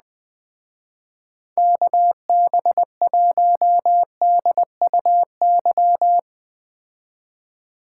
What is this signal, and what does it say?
Morse code, heard as a keyed tone, about 19 wpm, 700 Hz. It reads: KB1DUY